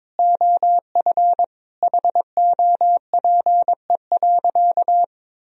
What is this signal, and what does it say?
Morse code, heard as a keyed tone, about 22 wpm, 700 Hz. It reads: OF HOPE.
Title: Sentences from Top 1000 Words - Review Part 3 - 22wpm